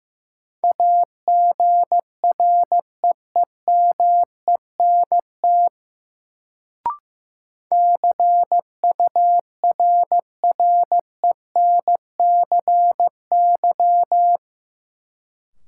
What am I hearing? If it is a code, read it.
Morse code, heard as a keyed tone, about 15 wpm, 700 Hz. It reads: AGREEMENT CURRENCY
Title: Sentences from Top 1500 Words - New Word Drill - 15wpm